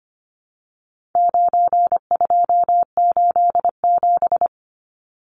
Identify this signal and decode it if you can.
Morse code, heard as a keyed tone, about 25 wpm, 700 Hz. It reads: 9287
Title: Numbers - 4-Digit 25wpm